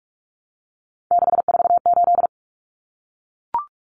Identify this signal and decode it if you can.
Morse code, heard as a keyed tone, about 44 wpm, 700 Hz. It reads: 648
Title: Numbers - 3-Digit 45wpm